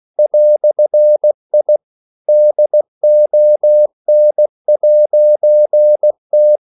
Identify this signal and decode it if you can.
Morse code, heard as a keyed tone, about 16 wpm, 600 Hz. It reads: "I DON'T